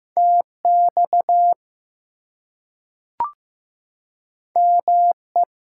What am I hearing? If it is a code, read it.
Morse code, heard as a keyed tone, about 15 wpm, 700 Hz. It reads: TX ME
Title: US States and Canadian Provinces 15wpm